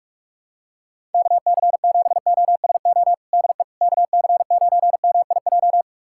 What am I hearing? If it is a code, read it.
Morse code, heard as a keyed tone, about 45 wpm, 700 Hz. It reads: KC7OSO DE KC9MIJ
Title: USA Callsign DE Callsign - 6 Characters - 45wpm